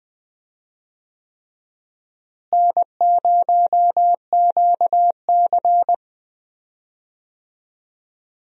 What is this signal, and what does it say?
Morse code, heard as a keyed tone, about 20 wpm, 700 Hz. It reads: N0QC
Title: USA Call Signs - 4 Characters - Repeated 3x - 20wpm